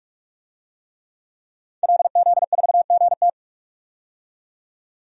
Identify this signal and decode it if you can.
Morse code, heard as a keyed tone, about 41 wpm, 700 Hz. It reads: LZ4GT
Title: DXCC-100 Call Signs - 5 Characters - 45wpm